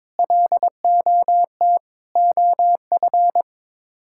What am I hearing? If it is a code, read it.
Morse code, heard as a keyed tone, about 22 wpm, 700 Hz. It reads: LOT OF
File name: practice / 22wpm / Sets of 6 Words - 22wpm